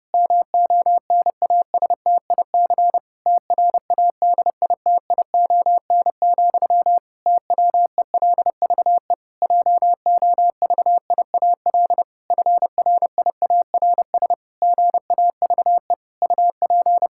Morse code, heard as a keyed tone, about 30 wpm, 700 Hz. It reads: MONASTIC TRADITION, TWELVE JOVIAL FRIARS GAVE UP
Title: Pangram Challenge v2 - 30wpm